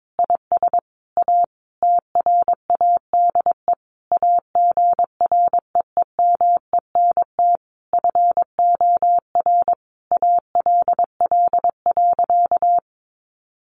Morse code, heard as a keyed tone, about 22 wpm, 700 Hz. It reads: IS A TRADE AGREEMENT FOR ALL.
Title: Sentences from Top 1500 Words 22wpm